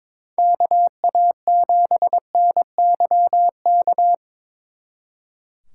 Morse code, heard as a keyed tone, about 26 wpm, 700 Hz. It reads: KA7NYK
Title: Speed Racing - USA 2x3 Call Signs 22wpm